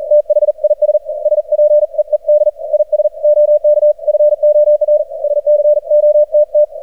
Morse code, heard as a keyed tone, about 44 wpm, 600 Hz. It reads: THIS SPEED IS OK FOR SCOTT
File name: archived / CRSnet